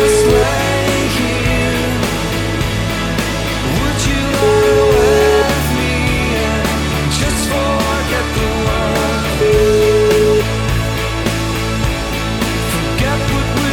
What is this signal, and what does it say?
Ringback tone (Europe, 425 Hz), with noise.